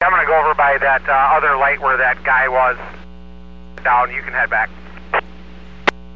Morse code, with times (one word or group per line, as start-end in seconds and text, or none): none